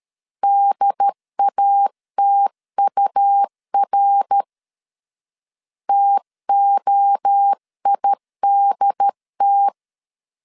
Morse code, heard as a keyed tone, about 13 wpm, 800 Hz. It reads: DATUR TOIDT
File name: CW / DET / 1 3 1500C S A 12wpm 2905